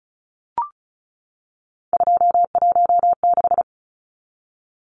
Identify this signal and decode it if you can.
Morse code, heard as a keyed tone, about 34 wpm, 700 Hz. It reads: 216
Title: Numbers - 3-Digit 35wpm